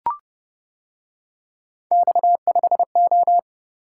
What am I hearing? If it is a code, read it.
Morse code, heard as a keyed tone, about 29 wpm, 700 Hz. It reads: X5O